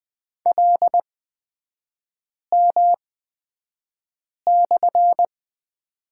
Morse code, heard as a keyed tone, about 20 wpm, spaced as for 5 wpm, 700 Hz. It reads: LM/
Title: 255 Lesson 41e - Period - Three Characters 20wpm